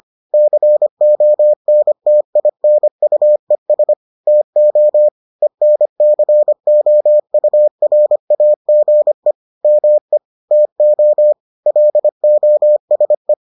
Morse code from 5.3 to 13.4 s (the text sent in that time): ENCOURAGE ME TO LOSE